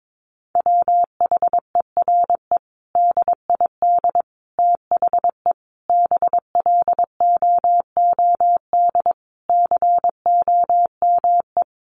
Morse code, heard as a keyed tone, about 22 wpm, 700 Hz. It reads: WHERE DID THE BLOOD COME
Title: Sentences from Top 1100 Words 22wpm